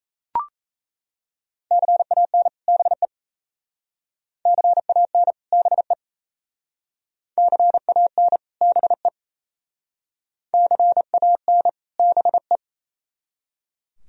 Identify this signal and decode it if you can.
Morse code, heard as a keyed tone, about 35 wpm, 700 Hz. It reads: CAN BE CAN BE CAN BE CAN BE
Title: Speed-Racing - Sets of 2 Words 28wpm